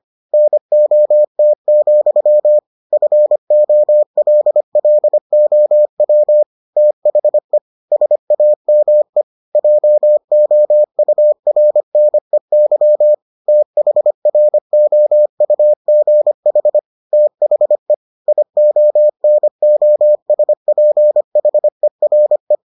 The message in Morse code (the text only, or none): NOT, FOLLOW THE SAME JOURNEY THROUGH THE IONOSPHERE